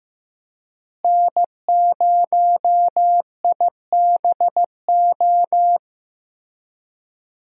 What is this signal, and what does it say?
Morse code, heard as a keyed tone, about 15 wpm, 700 Hz. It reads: N0IBO
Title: USA Call Signs - 5 Characters - Repeated 3x - 15wpm